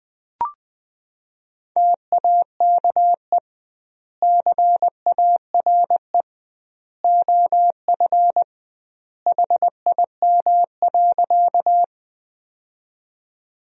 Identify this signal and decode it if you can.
Morse code, heard as a keyed tone, about 20 wpm, 700 Hz. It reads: TAKE CARE OF HIM.
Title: 2x Word-Spacing - Sentences from Top 300 Words - 20wpm